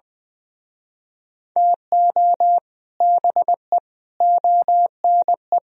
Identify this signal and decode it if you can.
Morse code, heard as a keyed tone, about 20 wpm, 700 Hz. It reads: TO BE ONE
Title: Speed-Racing - Sets of 4 Words 17wpm